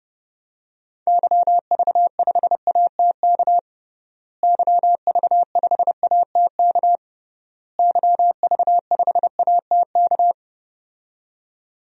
Morse code, heard as a keyed tone, about 30 wpm, 700 Hz. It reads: YV5ATK YV5ATK YV5ATK
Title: DXCC-100 Call Signs - 6 Characters - Repeated 3x - 30wpm